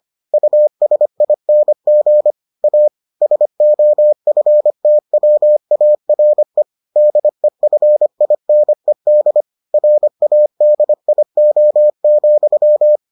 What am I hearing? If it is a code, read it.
Morse code, heard as a keyed tone, about 25 wpm, 600 Hz. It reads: USING A SOFTWARE DEFINED RADIO,